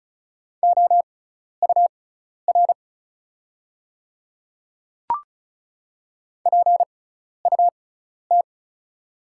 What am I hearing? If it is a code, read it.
Morse code, heard as a keyed tone, about 36 wpm, spaced as for 12 wpm, 690 Hz. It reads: OUR PUT
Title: 6x Character Spacing - Top 500 Words - 35wpm